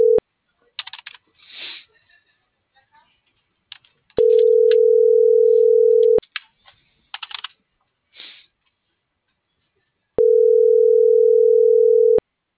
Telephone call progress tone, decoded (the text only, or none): ringback tone